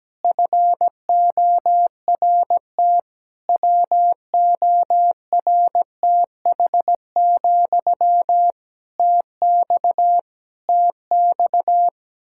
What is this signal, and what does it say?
Morse code, heard as a keyed tone, about 17 wpm, 700 Hz. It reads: FORT WORTH, TX TX
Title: Practice QSO - Exchange 1 - Beginning - 17wpm